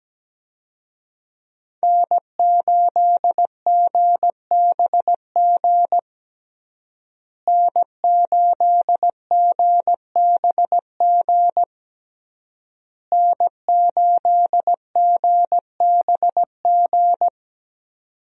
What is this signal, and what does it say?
Morse code, heard as a keyed tone, about 17 wpm, 700 Hz. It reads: N8GBG N8GBG N8GBG
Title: USA Call Signs - 5 Characters - Repeated 3x - 17wpm